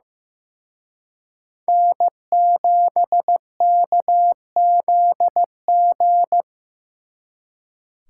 Morse code, heard as a keyed tone, about 18 wpm, 700 Hz. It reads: N7KZG
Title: Speed Racing - USA Call Signs 15wpm